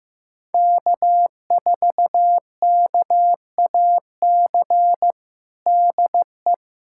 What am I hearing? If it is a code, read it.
Morse code, heard as a keyed tone, about 15 wpm, 700 Hz. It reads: K4KAC DE
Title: USA Callsign DE Callsign 15wpm